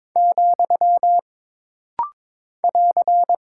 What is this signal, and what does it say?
Morse code, heard as a keyed tone, about 22 wpm, 700 Hz. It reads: , <AR>
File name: practice / 22wpm / Single Punctuation-Prosign - Rapid-Fire - 22wpm